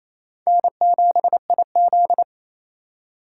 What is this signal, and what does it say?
Morse code, heard as a keyed tone, about 34 wpm, 700 Hz. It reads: N7IZ
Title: Speed Racing - USA 1x2 Call Signs 28wpm